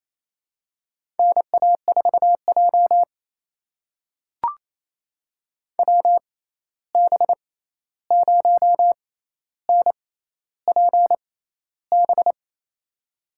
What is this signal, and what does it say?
Morse code, heard as a keyed tone, about 27 wpm, spaced as for 13 wpm, 700 Hz. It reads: NA4J WB0NPB
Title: USA Call Signs - 6x Character Spacing - 28wpm